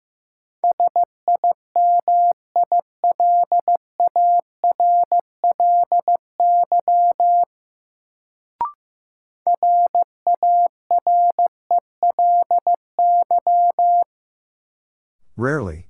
Morse code, heard as a keyed tone, about 15 wpm, 700 Hz. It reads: SIMILARLY RARELY
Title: Sentences from Top 1400 Words - New Word Drill - 15wpm